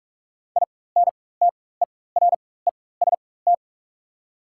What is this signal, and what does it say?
Morse code, heard as a keyed tone, about 43 wpm, spaced as for 21 wpm, 700 Hz. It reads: INTEREST
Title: 4x Character Spacing - Top 500 Words - 45wpm